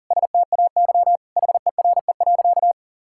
Morse code, heard as a keyed tone, about 40 wpm, 700 Hz. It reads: STAY HERE.
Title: Sentences from Top 1600 Words 40wpm